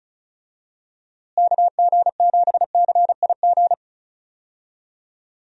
Morse code, heard as a keyed tone, about 36 wpm, 700 Hz. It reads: KG7CIG
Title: USA Call Signs - 6 Characters - Repeated 2x - 35wpm